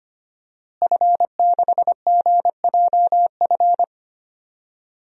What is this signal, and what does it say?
Morse code, heard as a keyed tone, about 25 wpm, 700 Hz. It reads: F6GJF
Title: DXCC-100 Call Signs - 5 Characters - Repeated 3x - 25wpm